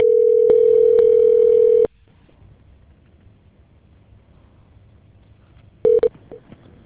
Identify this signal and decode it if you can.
Ringback tone (North America), 440 + 480 Hz.